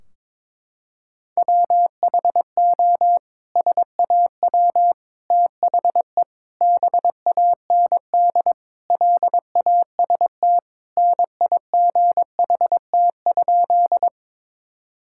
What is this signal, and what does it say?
Morse code, heard as a keyed tone, about 22 wpm, 700 Hz. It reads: WHO SAW THE BAND LAST NIGHT?
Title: Sentences from Top 1600 Words 22wpm